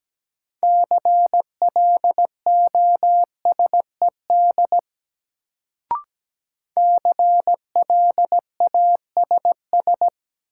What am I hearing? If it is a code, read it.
Morse code, heard as a keyed tone, about 17 wpm, 700 Hz. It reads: CLOSED CLASS
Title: Sentences from Top 1400 Words - New Word Drill Alt - 17wpm